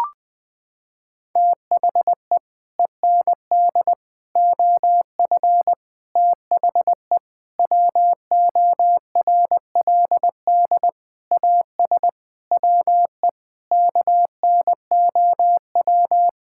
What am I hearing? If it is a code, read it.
Morse code, heard as a keyed tone, about 21 wpm, 700 Hz. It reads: THE END OF THE WORLD AS WE KNOW
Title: Sets of 9 Words - 20wpm